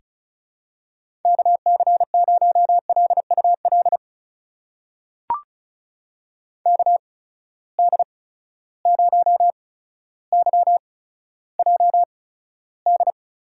Morse code, heard as a keyed tone, about 34 wpm, spaced as for 15 wpm, 700 Hz. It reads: KC0LUL KD0YJD
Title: USA Call Signs - 6 Characters - 8x Character Spacing - 35wpm